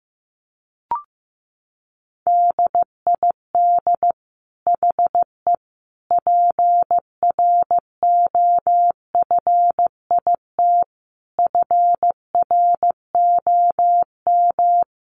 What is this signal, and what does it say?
Morse code, heard as a keyed tone, about 15 wpm, 700 Hz. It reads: DID HE PROFIT FROM
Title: Sentences from Top 800 Words 15wpm